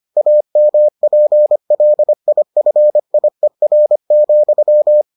Morse code, heard as a keyed tone, about 25 wpm, 600 Hz. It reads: AMPLIFIER,